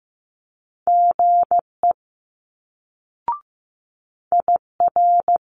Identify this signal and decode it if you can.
Morse code, heard as a keyed tone, about 15 wpm, 700 Hz. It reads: GE IR
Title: Multi-letter Phonemes Quick Fire 15wpm